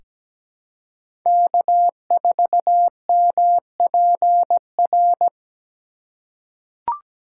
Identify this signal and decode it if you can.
Morse code, heard as a keyed tone, about 17 wpm, 700 Hz. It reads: K4MPR
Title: USA Call Signs - 5 Characters - 4x Character Spacing - 17wpm